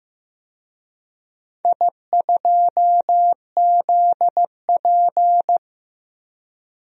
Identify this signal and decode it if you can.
Morse code, heard as a keyed tone, about 15 wpm, 700 Hz. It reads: I2ZP
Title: DXCC-100 Call Signs - 4 Characters - 15wpm